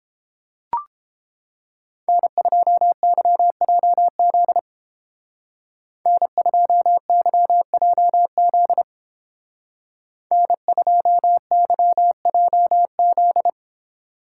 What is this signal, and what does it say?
Morse code, heard as a keyed tone, about 28 wpm, 700 Hz. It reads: N2YJZ N2YJZ N2YJZ